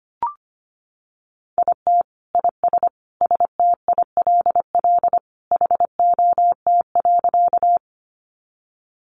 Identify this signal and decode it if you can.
Morse code, heard as a keyed tone, about 25 wpm, 700 Hz. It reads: IT IS STILL HOT.
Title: Additional Easy Phrases 25wpm